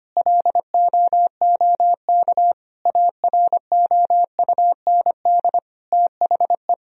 Morse code, heard as a keyed tone, about 25 wpm, 700 Hz. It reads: LOOK AROUND THE
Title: Sentences from Top 1000 Words - Review Part 3 - 25wpm